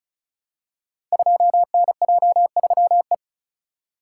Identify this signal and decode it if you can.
Morse code, heard as a keyed tone, about 35 wpm, 700 Hz. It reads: 2NJ3E
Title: Code Groups - 5 Letter:Numbers - 35wpm